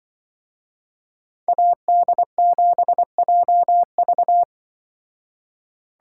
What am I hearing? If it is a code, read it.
Morse code, heard as a keyed tone, about 22 wpm, 700 Hz. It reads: AD7JV
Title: Speed Racing - USA 2x2 Call Signs 20wpm